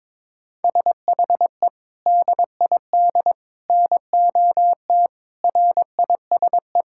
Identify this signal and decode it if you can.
Morse code, heard as a keyed tone, about 22 wpm, 700 Hz. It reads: SHE DID NOT RISE
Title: Sentences from Top 1000 Words - Review Part 6 - 22wpm